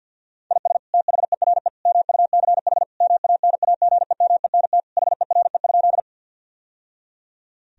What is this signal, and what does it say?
Morse code, heard as a keyed tone, about 50 wpm, 700 Hz. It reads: IS THERE MUCH MANAGEMENT HERE?
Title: Sentences from Top 1000 Words - Review Part 4 - 50wpm